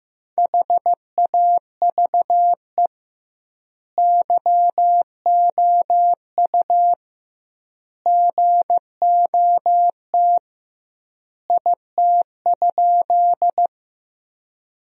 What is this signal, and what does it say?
Morse code, heard as a keyed tone, about 15 wpm, 700 Hz. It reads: HAVE YOU GOT IT?
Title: 2x Word-Spacing - Sentences from Top 300 Words - 15wpm